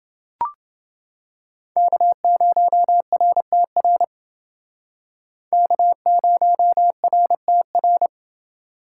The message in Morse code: K0RTR K0RTR

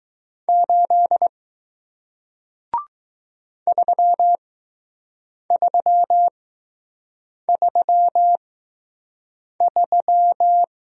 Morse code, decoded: 8 3333

